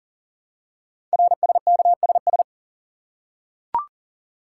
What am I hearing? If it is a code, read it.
Morse code, heard as a keyed tone, about 38 wpm, 690 Hz. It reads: RSKSS
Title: Code Groups - 5 Letters - 40wpm